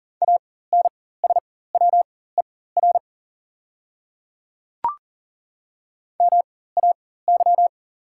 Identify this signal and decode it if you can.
Morse code, heard as a keyed tone, about 40 wpm, spaced as for 19 wpm, 700 Hz. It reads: ANSWER MAY
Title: 4x Character Spacing - Top 300 Words - 40wpm